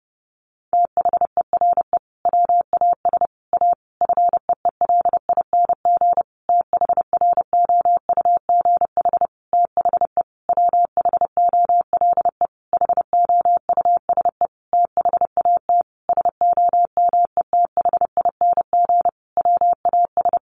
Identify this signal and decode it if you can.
Morse code, heard as a keyed tone, about 28 wpm, 700 Hz. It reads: THERE WAS A FEELING THROUGH THE WHOLE HOUSE THAT SOMETHING WAS
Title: Sentences from Top 1200 Words 30wpm